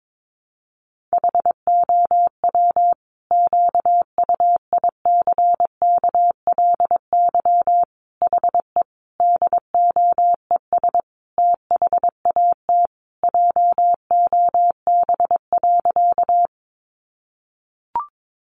Morse code, heard as a keyed tone, about 22 wpm, 700 Hz. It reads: HOW QUICKLY HE DOES THAT JOB.
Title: Sentences from Top 700 Words 22wpm